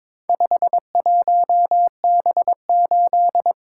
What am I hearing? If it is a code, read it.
Morse code, heard as a keyed tone, about 22 wpm, 700 Hz. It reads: 51B8
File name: practice / 22wpm / Four Letter-Number Combinations - 22wpm